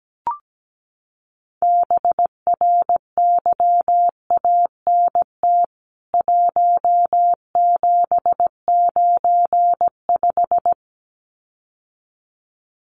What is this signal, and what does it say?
Morse code, heard as a keyed tone, about 17 wpm, 700 Hz. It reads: BRYANT 1795
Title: CWT Exchange 17wpm